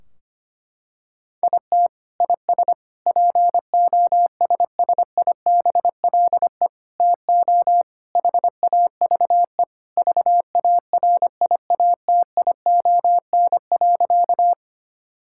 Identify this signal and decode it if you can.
Morse code, heard as a keyed tone, about 25 wpm, 700 Hz. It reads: IT IS POSSIBLE TO HAVE VARIATION.
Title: Sentences from Top 1600 Words 25wpm